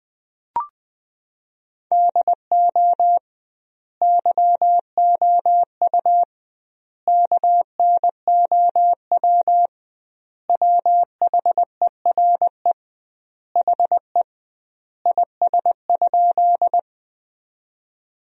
Morse code, heard as a keyed tone, about 21 wpm, 700 Hz. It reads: DO YOU KNOW WHERE HE IS?
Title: 2x Word-Spacing - Sentences from Top 300 Words - 20wpm